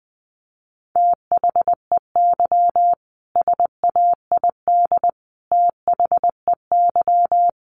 Morse code, heard as a keyed tone, about 20 wpm, 700 Hz. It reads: THEY SAID THEY
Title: Sentences from Top 1400 Words 20wpm